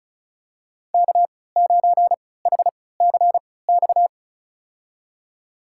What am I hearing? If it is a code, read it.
Morse code, heard as a keyed tone, about 36 wpm, spaced as for 20 wpm, 700 Hz. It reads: K9HCX
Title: USA Call Signs - 5 Characters - 3x Character Spacing - 35wpm